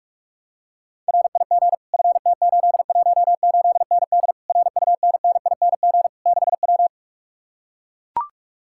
Morse code, heard as a keyed tone, about 44 wpm, 700 Hz. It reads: RIG FT818ND RUNNING 6W